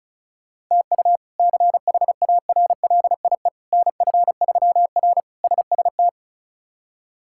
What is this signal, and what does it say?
Morse code, heard as a keyed tone, about 34 wpm, 700 Hz. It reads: TU CHARLIE NF3R SST
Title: K1USN SST QSO Practice - 35wpm